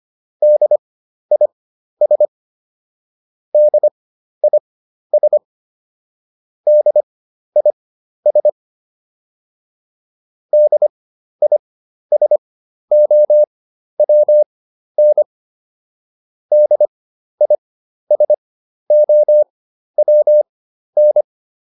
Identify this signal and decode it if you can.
Morse code, heard as a keyed tone, about 25 wpm, spaced as for 12 wpm, 600 Hz. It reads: DIS DIS DIS DISOWN DISOWN